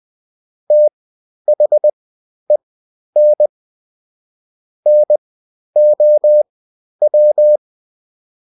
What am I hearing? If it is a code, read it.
Morse code, heard as a keyed tone, about 20 wpm, spaced as for 11 wpm, 600 Hz. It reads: THEN NOW
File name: Top-100-Words-10